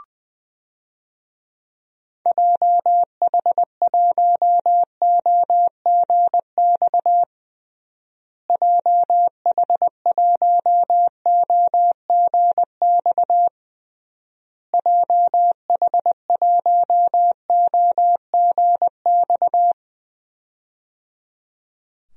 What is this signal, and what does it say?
Morse code, heard as a keyed tone, about 20 wpm, 700 Hz. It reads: JH1OGX JH1OGX JH1OGX
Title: DXCC-100 Call Signs - 6 Characters - Repeated 3x - 20wpm